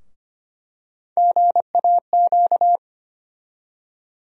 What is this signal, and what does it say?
Morse code, heard as a keyed tone, about 25 wpm, 700 Hz. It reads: GAQ
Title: USA Call Sign Postfixes - 3 Letters 25wpm